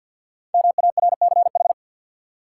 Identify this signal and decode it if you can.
Morse code, heard as a keyed tone, about 52 wpm, 700 Hz. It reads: MARCH
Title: Five Letter Words - 50wpm